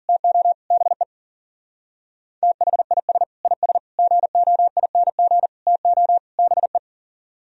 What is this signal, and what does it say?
Morse code, heard as a keyed tone, about 49 wpm, 700 Hz. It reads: TO BE THIS IS GOING TO BE
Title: Speed-Racing - Sets of 5 Words 40wpm